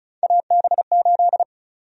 Morse code, heard as a keyed tone, about 36 wpm, 700 Hz. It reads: AB8
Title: USA Call Sign Prefixes - 35wpm